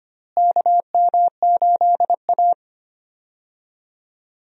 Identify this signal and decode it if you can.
Morse code, heard as a keyed tone, about 25 wpm, 700 Hz. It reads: KM8A
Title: USA Call Signs - 4 Characters - Repeated 3x - 25wpm